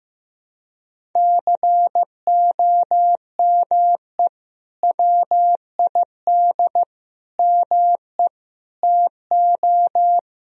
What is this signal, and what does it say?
Morse code, heard as a keyed tone, about 16 wpm, 700 Hz. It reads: COME WID ME TO
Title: Sentences from Top 200 Words and Abbreviations 15wpm